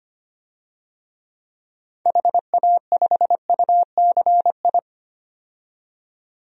Morse code, heard as a keyed tone, about 25 wpm, 700 Hz. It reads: HA5UCI
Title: DXCC-100 Call Signs - 6 Characters - 25wpm